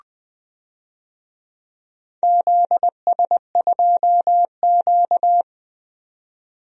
Morse code, heard as a keyed tone, about 20 wpm, 700 Hz. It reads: ZS2Q
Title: DXCC-100 Call Signs - 4 Characters - 20wpm